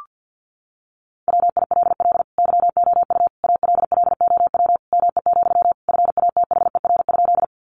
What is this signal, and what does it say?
Morse code, heard as a keyed tone, about 50 wpm, 700 Hz. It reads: WILL YOU ALLOW ME, FATHER?
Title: Sentences from Top 700 Words 50wpm